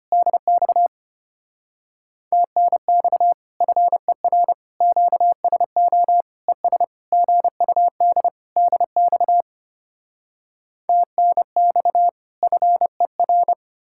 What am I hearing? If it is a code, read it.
Morse code, heard as a keyed tone, about 33 wpm, 700 Hz. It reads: DX TNX FER QSO ES GUD DX TNX FER